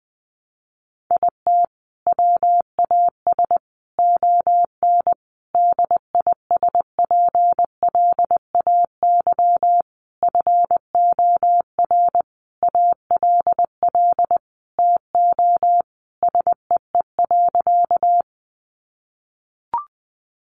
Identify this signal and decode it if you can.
Morse code, heard as a keyed tone, about 20 wpm, 700 Hz. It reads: IT WAS ON DISPLAY FOR ALL TO SEE.
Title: Sentences from Top 1200 Words 20wpm